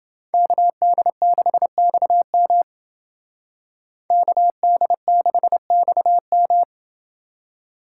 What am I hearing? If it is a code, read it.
Morse code, heard as a keyed tone, about 29 wpm, 700 Hz. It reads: KD6XM KD6XM